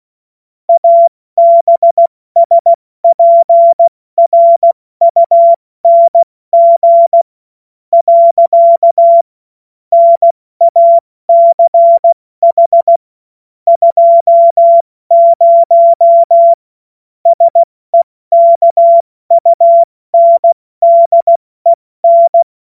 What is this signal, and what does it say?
Morse code, heard as a keyed tone, about 16 wpm, spaced as for 14 wpm, 680 Hz. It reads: ABSPRUNG. NACH 20 SEKUNDEN
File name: Morsekurs / Witze / Witz 31.Oktober 2023 70 F80